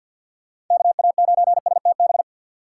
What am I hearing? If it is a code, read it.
Morse code, heard as a keyed tone, about 51 wpm, 700 Hz. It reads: KA9STB
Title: USA Call Signs - 6 Characters - 3x Character Spacing - 50wpm